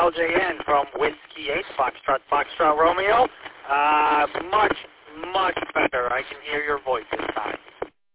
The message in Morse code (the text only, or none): none